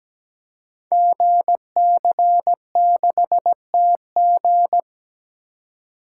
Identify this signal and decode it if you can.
Morse code, heard as a keyed tone, about 17 wpm, 700 Hz. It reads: GC6TG
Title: Code Groups - 5 Letter:Numbers - 17wpm